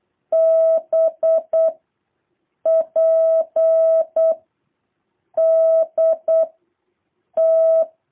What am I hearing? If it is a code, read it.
Morse code, heard as a keyed tone, about 8 wpm, 660 Hz. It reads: B P D T
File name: jh8cbh / 2023cw